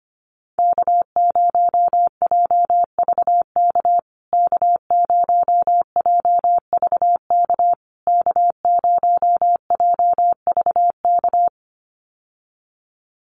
Morse code, heard as a keyed tone, about 25 wpm, 700 Hz. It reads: K0JVK K0JVK K0JVK
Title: USA Call Signs Repeated 3x 25wpm